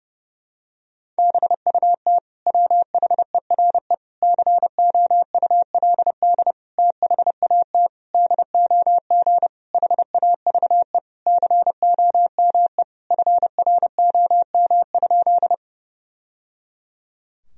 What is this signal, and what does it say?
Morse code, heard as a keyed tone, about 32 wpm, 700 Hz. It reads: BUT WHERE COULD THAT DOG HAVE COME FROM?
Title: Sentences from Top 1000 Words - Review Part 2 - 30wpm